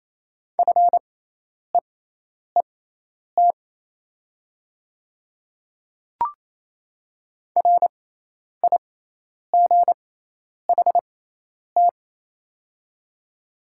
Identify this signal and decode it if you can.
Morse code, heard as a keyed tone, about 27 wpm, spaced as for 10 wpm, 700 Hz. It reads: FEET RIGHT
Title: 6x Character Spacing - Top 300 Words - 28wpm